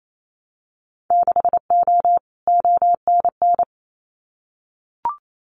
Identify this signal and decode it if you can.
Morse code, heard as a keyed tone, about 28 wpm, 700 Hz. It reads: 6O ONN
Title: Winter Field Day Exchanges Encore 28wpm